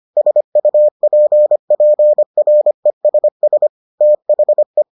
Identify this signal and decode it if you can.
Morse code, heard as a keyed tone, about 25 wpm, 600 Hz. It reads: SUPPRESS THE